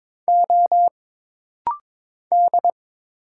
Morse code, heard as a keyed tone, about 22 wpm, 700 Hz. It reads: O D
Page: This rapid-fire timing is geared to push your ICR (Instant Character Recognition) skills.